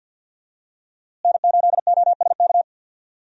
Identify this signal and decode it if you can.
Morse code, heard as a keyed tone, about 48 wpm, 700 Hz. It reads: N8OSK
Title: USA Call Signs - 5 Characters - Repeated 2x - 50wpm